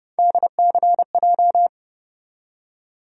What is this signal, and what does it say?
Morse code, heard as a keyed tone, about 30 wpm, 700 Hz. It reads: DCJ
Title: USA Call Sign Postfixes - 3 Letters 30wpm